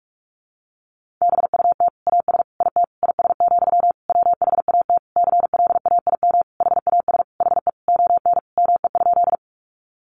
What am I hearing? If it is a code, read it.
Morse code, heard as a keyed tone, about 44 wpm, 700 Hz. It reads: BUT AS IT IS, WHAT CLAIM HAS HE ON ME?
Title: Sentences from Top 1500 Words 45wpm